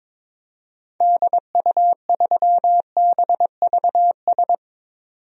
Morse code, heard as a keyed tone, about 22 wpm, 700 Hz. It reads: DU3BVS
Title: DXCC-100 Call Signs - 6 Characters - 22wpm